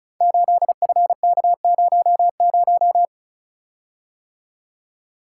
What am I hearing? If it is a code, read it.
Morse code, heard as a keyed tone, about 36 wpm, 700 Hz. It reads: 8FK00
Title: Code Groups - 5 Letter:Numbers - 35wpm